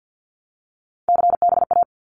Morse code, heard as a keyed tone, about 50 wpm, 700 Hz. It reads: CBA
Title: Abbreviations Repeated 3x - 50wpm